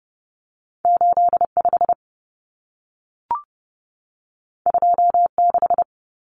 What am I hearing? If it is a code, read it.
Morse code, heard as a keyed tone, about 30 wpm, 700 Hz. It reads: 85 26
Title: Numbers - 2-Digit 30wpm